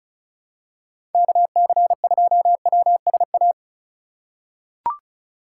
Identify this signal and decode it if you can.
Morse code, heard as a keyed tone, about 36 wpm, 700 Hz. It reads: KC2WSA
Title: USA Call Signs - 3x Character Spacing - 35wpm